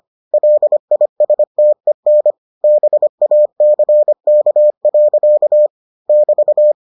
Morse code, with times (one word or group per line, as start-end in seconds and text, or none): 0.3-2.3 LISTEN
2.6-5.7 BACK.
6.1-6.7 <BT>